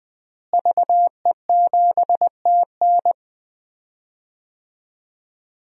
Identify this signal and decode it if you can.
Morse code, heard as a keyed tone, about 20 wpm, 700 Hz. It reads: VE7TN
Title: DXCC-100 Call Signs - 5 Characters - Repeated 3x - 20wpm